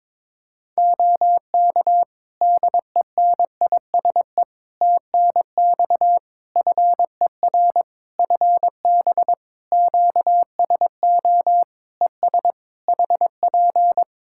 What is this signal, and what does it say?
Morse code, heard as a keyed tone, about 22 wpm, 700 Hz. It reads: OK DENISE TNX FER FB QSO ES HP
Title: K1USN SST QSO Practice 22wpm